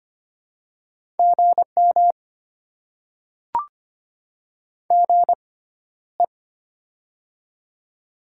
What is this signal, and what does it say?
Morse code, heard as a keyed tone, about 25 wpm, spaced as for 13 wpm, 700 Hz. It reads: GM GE
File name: Top 20 QSO Elements - 6x Character Spacing - 25wpm